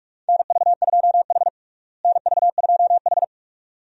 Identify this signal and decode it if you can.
Morse code, heard as a keyed tone, about 41 wpm, 700 Hz. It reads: NV2H NV2H
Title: USA Call Signs - 4 Characters - Repeated 2x - 45wpm